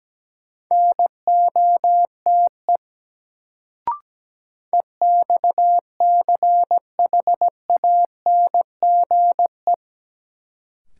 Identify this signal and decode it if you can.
Morse code, heard as a keyed tone, about 17 wpm, 700 Hz. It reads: NOTE EXCHANGE
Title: Sets of 5 Words - New Word Drill - 17wpm